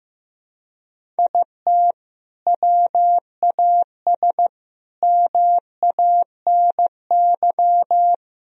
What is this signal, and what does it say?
Morse code, heard as a keyed tone, about 15 wpm, 700 Hz. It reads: IT WAS MANY
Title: Sentences from Top 1000 Words - Review Part 1 - 15wpm